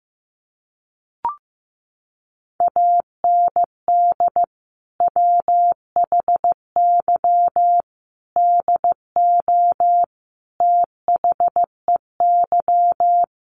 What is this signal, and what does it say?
Morse code, heard as a keyed tone, about 15 wpm, 700 Hz. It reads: AND WHY DO THEY
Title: Sentences from Top 1100 Words 15wpm